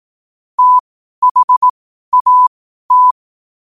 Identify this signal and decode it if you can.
Morse code, heard as a keyed tone, about 18 wpm, spaced as for 12 wpm, 1000 Hz. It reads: THAT
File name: qotdfiles / 12wpm